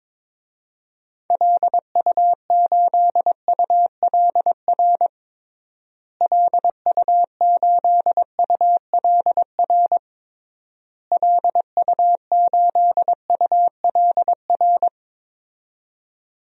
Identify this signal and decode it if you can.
Morse code, heard as a keyed tone, about 22 wpm, 700 Hz. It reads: LU8ULR LU8ULR LU8ULR
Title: DXCC-100 Call Signs - 6 Characters - Repeated 3x - 22wpm